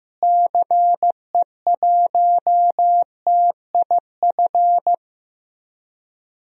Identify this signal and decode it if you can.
Morse code, heard as a keyed tone, about 16 wpm, 700 Hz. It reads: CE1TIF
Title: DXCC-100 Call Signs - 6 Characters - Repeated 2x - 15wpm